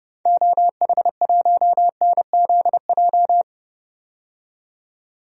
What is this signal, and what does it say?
Morse code, heard as a keyed tone, about 31 wpm, 700 Hz. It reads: OH1NZJ
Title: DXCC-100 Call Signs - 6 Characters - 30wpm